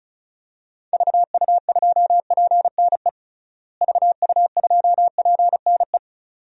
Morse code, heard as a keyed tone, about 34 wpm, 700 Hz. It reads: VU2PNE VU2PNE
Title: DXCC-100 Call Signs - 6 Characters - Repeated 2x - 35wpm